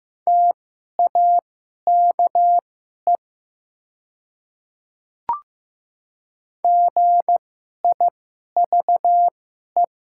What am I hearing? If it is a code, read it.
Morse code, heard as a keyed tone, about 15 wpm, spaced as for 11 wpm, 700 Hz. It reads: TAKE GIVE
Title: 2x Character Spacing - Top 300 Words - 15wpm